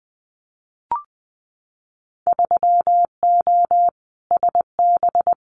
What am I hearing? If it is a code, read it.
Morse code, heard as a keyed tone, about 20 wpm, 700 Hz. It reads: 3O SB
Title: Winter Field Day Exchanges Encore 20wpm